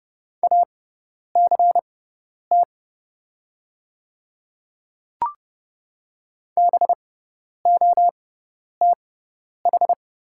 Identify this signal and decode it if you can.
Morse code, heard as a keyed tone, about 30 wpm, spaced as for 10 wpm, 700 Hz. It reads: ACT BOTH
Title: 6x Character Spacing - Top 300 Words - 30wpm